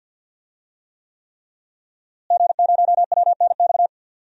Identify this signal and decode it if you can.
Morse code, heard as a keyed tone, about 46 wpm, 700 Hz. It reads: G0WNX
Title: DXCC-100 Call Signs - 5 Characters - Repeated 3x - 50wpm